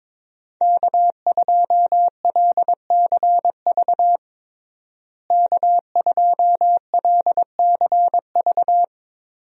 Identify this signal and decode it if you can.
Morse code, heard as a keyed tone, about 22 wpm, 700 Hz. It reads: K2LCV K2LCV
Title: USA Call Signs - 5 Characters - Repeated 2x - 22wpm